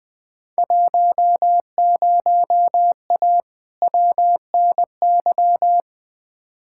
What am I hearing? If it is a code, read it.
Morse code, heard as a keyed tone, about 20 wpm, 700 Hz. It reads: 10A WNY
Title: Field Day Exchanges - 20wpm